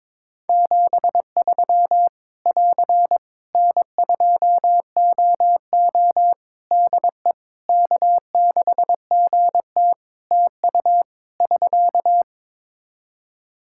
Morse code, heard as a keyed tone, about 22 wpm, 700 Hz. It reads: 73 <AR> N2OO DE K6GT TU <SK>
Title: Practice QSO - Exchange 4 - Ending - 22wpm